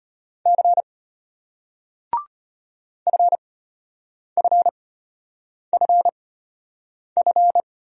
Morse code, heard as a keyed tone, about 34 wpm, spaced as for 8 wpm, 700 Hz. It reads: C FFFF